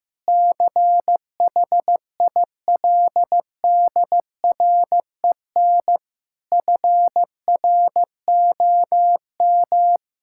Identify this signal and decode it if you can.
Morse code, heard as a keyed tone, about 15 wpm, 700 Hz. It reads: CHILDREN FROM
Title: Sentences from Top 1000 Words - Review Part 6 - 15wpm